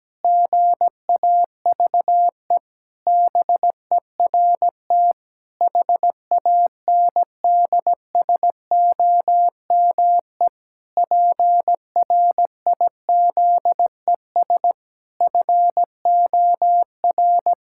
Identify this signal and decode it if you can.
Morse code, heard as a keyed tone, about 17 wpm, 700 Hz. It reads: GAVE BERT HANDSOME PRIZES FOR
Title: Pangram Challenge v2 - 17wpm